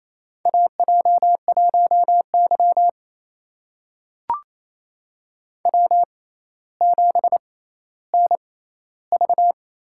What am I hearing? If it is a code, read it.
Morse code, heard as a keyed tone, about 28 wpm, spaced as for 14 wpm, 700 Hz. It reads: AJ1Y W7NV
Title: USA Call Signs - 4 Characters - 6x Character Spacing - 28wpm